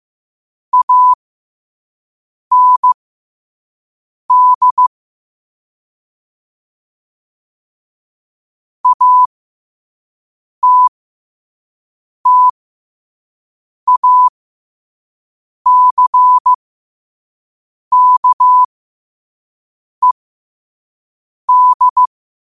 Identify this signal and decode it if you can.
Morse code, heard as a keyed tone, about 15 wpm, spaced as for 5 wpm, 1000 Hz. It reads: AND ATTACKED
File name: qotdfiles / 5wpm